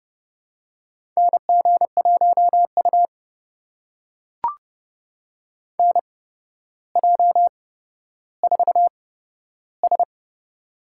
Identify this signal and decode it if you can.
Morse code, heard as a keyed tone, about 30 wpm, spaced as for 13 wpm, 700 Hz. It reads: NG1U NJ4S
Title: USA Call Signs - 4 Characters - 8x Character Spacing - 30wpm